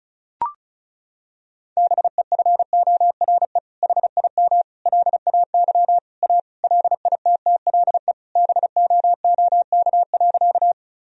Morse code, heard as a keyed tone, about 36 wpm, 700 Hz. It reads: BEFORE HIM LAY A LITTLE BOOK.
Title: Sentences from Top 1000 Words - Review Part 2 - 35wpm